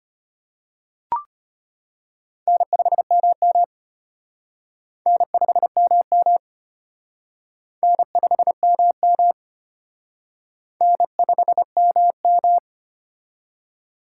Morse code, heard as a keyed tone, about 31 wpm, 700 Hz. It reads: N5MM N5MM N5MM N5MM
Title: Speed Racing - USA 1x2 Call Signs 25wpm